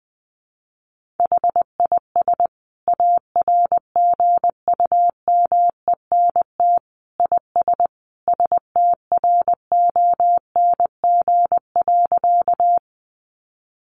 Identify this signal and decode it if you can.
Morse code, heard as a keyed tone, about 19 wpm, 700 Hz. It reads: HIS ARGUMENT IS STRONG.
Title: Sentences from Top 1400 Words 20wpm